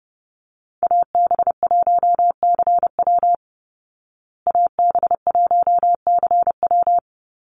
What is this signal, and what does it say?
Morse code, heard as a keyed tone, about 30 wpm, 700 Hz. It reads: AB1CW AB1CW
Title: Call Signs Repeated 3x - 30wpm